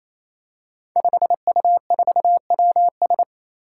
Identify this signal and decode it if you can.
Morse code, heard as a keyed tone, about 28 wpm, 700 Hz. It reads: 5U4WS
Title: Five Challenging Letter-Number Combinations 28wpm